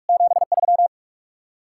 Morse code, heard as a keyed tone, about 59 wpm, 700 Hz. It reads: 73